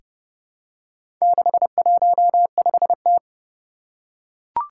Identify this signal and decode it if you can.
Morse code, heard as a keyed tone, about 28 wpm, 700 Hz. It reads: 615T
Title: Four Challenging Letter-Number Combinations 30wpm